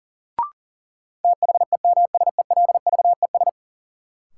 Morse code, heard as a keyed tone, about 40 wpm, 700 Hz. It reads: THEMSELVES
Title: Sentences from Top 800 Words - New Word Drill - 40wpm